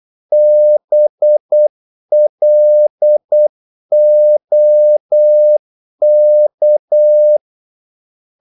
Morse code, heard as a keyed tone, about 8 wpm, 600 Hz. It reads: BLOK
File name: Korte ord DK 8wpm